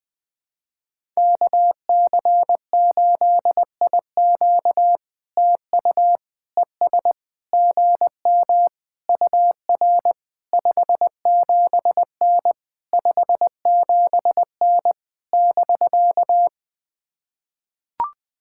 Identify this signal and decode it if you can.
Morse code, heard as a keyed tone, about 20 wpm, 700 Hz. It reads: KC8IQ TU ES GM UR 57N 57N <BK>
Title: POTA QSO Practice - 20wpm